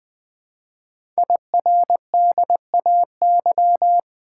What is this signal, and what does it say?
Morse code, heard as a keyed tone, about 20 wpm, 700 Hz. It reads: IRDAY
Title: Code Groups - 5 Letters - 20wpm